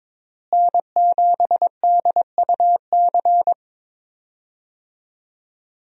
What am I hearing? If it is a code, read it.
Morse code, heard as a keyed tone, about 21 wpm, 700 Hz. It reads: N7DUC
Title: USA Call Signs - 5 Characters - Repeated 3x - 22wpm